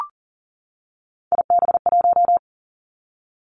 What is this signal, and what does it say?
Morse code, heard as a keyed tone, about 38 wpm, 700 Hz. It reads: IB1
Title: Three Challenging Letter-Number Combinations 40wpm